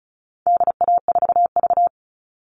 Morse code, heard as a keyed tone, about 34 wpm, 700 Hz. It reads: DA4V
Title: All Cut Numbers - 4-Digit 35wpm